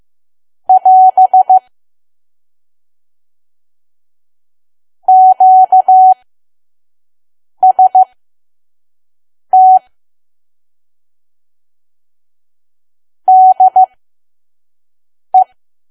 Morse code, heard as a keyed tone, about 15 wpm, spaced as for 5 wpm, 750 Hz. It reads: <AS> QST DE
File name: Archive / 5 WPM files / PM